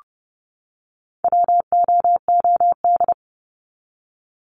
Morse code, heard as a keyed tone, about 30 wpm, 700 Hz. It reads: WOOD
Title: Top 300 Words 30wpm